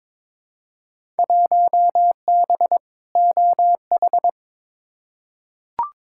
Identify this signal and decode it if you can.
Morse code, heard as a keyed tone, about 22 wpm, 700 Hz. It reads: 1B OH
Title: Field Day Exchanges - 22wpm